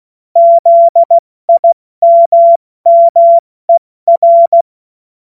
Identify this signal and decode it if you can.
Morse code, heard as a keyed tone, about 16 wpm, 680 Hz. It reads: ZIMMER
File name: Morsekurs / Witze / Witz 14. März 2023 70 F80